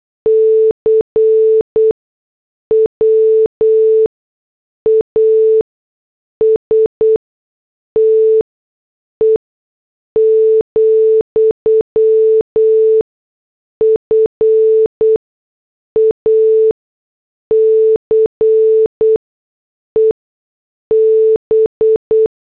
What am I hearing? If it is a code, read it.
Morse code, heard as a keyed tone, about 8 wpm, 440 Hz. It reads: C W A S T E , F A C E B
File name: morse / headlines